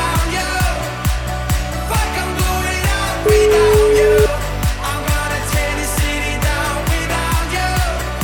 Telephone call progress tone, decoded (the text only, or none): ringback tone